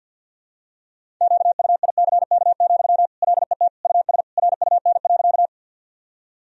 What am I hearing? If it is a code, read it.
Morse code, heard as a keyed tone, about 52 wpm, 700 Hz. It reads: QUICK, LET US RUN.
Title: Sentences from Top 1000 Words - Review Part 1 - 50wpm